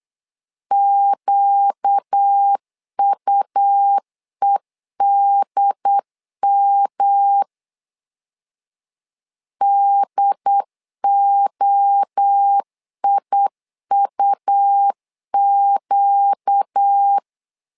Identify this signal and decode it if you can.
Morse code, heard as a keyed tone, about 8 wpm, 800 Hz. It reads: QUEDM DOIUQ
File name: 4 Q 8wpm